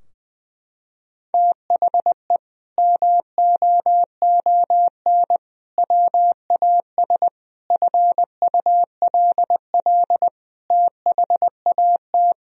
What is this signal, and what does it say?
Morse code, heard as a keyed tone, about 20 wpm, 700 Hz. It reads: THE MOON WAS FULL THAT
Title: Sentences from Top 1000 Words - Review Part 3 - 20wpm